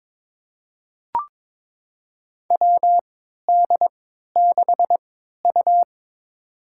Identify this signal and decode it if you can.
Morse code, heard as a keyed tone, about 22 wpm, spaced as for 13 wpm, 700 Hz. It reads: WD6U
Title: USA Call Signs - 4 Characters - 3x Character Spacing - 22wpm